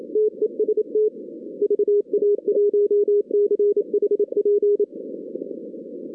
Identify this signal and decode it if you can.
Morse code, heard as a keyed tone, about 27 wpm, 440 Hz. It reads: TEST VA1CHP
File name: 2016 arrl dx cw / 2016-ARRL-DX-CW-K5TR- - 0-41